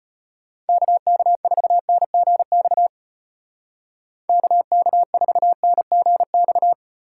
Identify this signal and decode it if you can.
Morse code, heard as a keyed tone, about 32 wpm, 700 Hz. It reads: KK4NGX KK4NGX